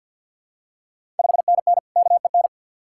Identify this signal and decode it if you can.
Morse code, heard as a keyed tone, about 51 wpm, 690 Hz. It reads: 5NN KEN
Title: Ontario QSO Party Exchange 50wpm